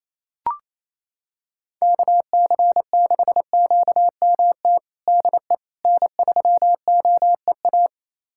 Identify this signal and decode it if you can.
Morse code, heard as a keyed tone, about 28 wpm, 700 Hz. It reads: KC6QMT DE N3OEA
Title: USA Callsign DE Callsign 28wpm